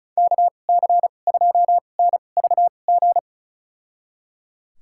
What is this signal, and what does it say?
Morse code, heard as a keyed tone, about 34 wpm, spaced as for 25 wpm, 700 Hz. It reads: KC2NVG
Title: USA Call Signs - 6 Characters - 2x Character Spacing - 35wpm